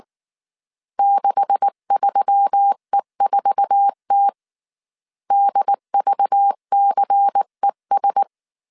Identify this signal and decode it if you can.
Morse code, heard as a keyed tone, about 19 wpm, 800 Hz. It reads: 63E4T DVCES
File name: CW / DET / 1 9 1500C S B 18wpm 2551